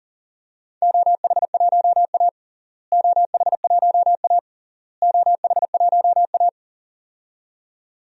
Morse code, heard as a keyed tone, about 37 wpm, 700 Hz. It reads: OH1A OH1A OH1A
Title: DXCC-100 Call Signs - 4 Characters - Repeated 3x - 40wpm